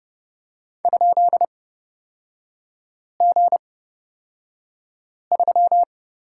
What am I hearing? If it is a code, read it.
Morse code, heard as a keyed tone, about 29 wpm, spaced as for 5 wpm, 700 Hz. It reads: ?G3
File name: Lesson 38e - Letter q - Three Characters 30wpm